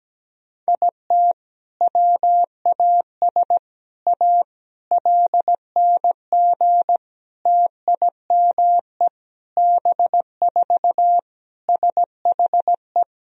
Morse code, heard as a keyed tone, about 17 wpm, 700 Hz. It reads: IT WAS A LNG TIME B4 SHE
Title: Sentences from Top 500 Words and Abbreviations 17wpm